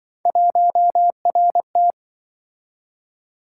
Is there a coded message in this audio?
Yes: Morse code from 0.3 to 1.9 s, heard as a keyed tone, about 26 wpm, 700 Hz. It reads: 1RT